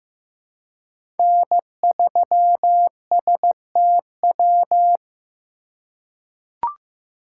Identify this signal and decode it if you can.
Morse code, heard as a keyed tone, about 15 wpm, 700 Hz. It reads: N3STW
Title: USA Call Signs - 5 Characters - 4x Character Spacing - 15wpm